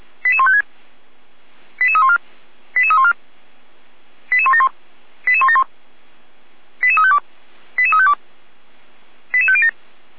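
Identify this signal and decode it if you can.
Five-tone selective call (ZVEI), heard as quick runs of five tones, tones of about 70 ms. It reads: 88156 88314 88314 88171 88171 88352 88352 88507
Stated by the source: ZVEI five-tone selective call 88080 received by SDR.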